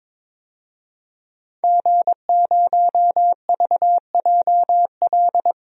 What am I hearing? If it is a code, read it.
Morse code, heard as a keyed tone, about 21 wpm, 700 Hz. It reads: G0VJL
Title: DXCC-100 Call Signs - 5 Characters - Repeated 3x - 22wpm